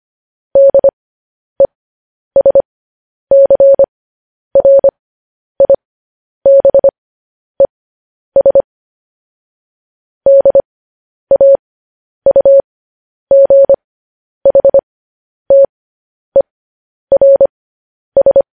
Morse code, heard as a keyed tone, about 25 wpm, spaced as for 10 wpm, 550 Hz. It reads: DESCRIBES DAUGHTERS
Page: news-headlines-in-morse-code-at-10-wpm